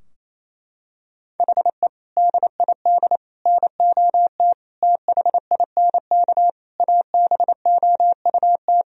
Morse code, heard as a keyed tone, about 28 wpm, 700 Hz. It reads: HE DID NOT THINK ABOUT
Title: Sentences from Top 1000 Words - Review Part 3 - 28wpm